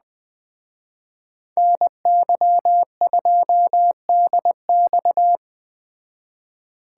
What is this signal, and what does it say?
Morse code, heard as a keyed tone, about 21 wpm, 700 Hz. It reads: NY2DX
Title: Speed Racing - USA 2x2 Call Signs 17wpm